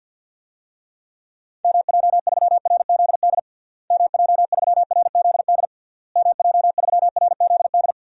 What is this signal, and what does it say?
Morse code, heard as a keyed tone, about 46 wpm, 700 Hz. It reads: MJ3RZD MJ3RZD MJ3RZD
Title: DXCC-100 Call Signs - 6 Characters - Repeated 3x - 50wpm